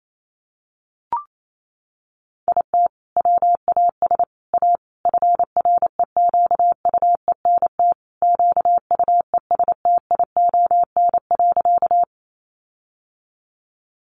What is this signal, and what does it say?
Morse code, heard as a keyed tone, about 28 wpm, 700 Hz. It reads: IT WAS A FREQUENT QUESTION.
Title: Sentences from Top 1400 Words 28wpm